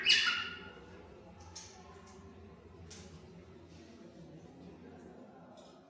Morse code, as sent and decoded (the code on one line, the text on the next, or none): none